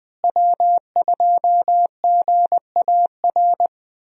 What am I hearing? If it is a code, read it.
Morse code, heard as a keyed tone, about 20 wpm, 700 Hz. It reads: W2GAR
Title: USA Call Signs - 5 Characters - Repeated 2x - 20wpm